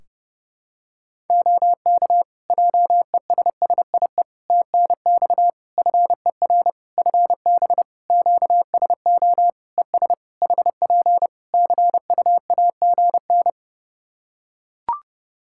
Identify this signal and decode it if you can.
Morse code, heard as a keyed tone, about 30 wpm, 700 Hz. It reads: OK JESSIE TNX FER FB QSO ES HP CUAGN
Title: Practice QSO - Exchange 4 - Ending - 30wpm